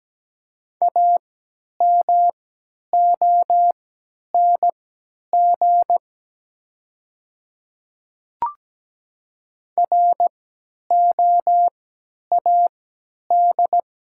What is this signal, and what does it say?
Morse code, heard as a keyed tone, about 17 wpm, spaced as for 10 wpm, 690 Hz. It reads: AMONG ROAD
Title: 3x Character Spacing - Top 500 Words - 17wpm